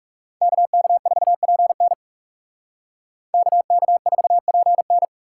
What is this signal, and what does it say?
Morse code, heard as a keyed tone, about 37 wpm, 700 Hz. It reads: KK4PN KK4PN
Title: Speed Racing - USA 2x2 Call Signs 30wpm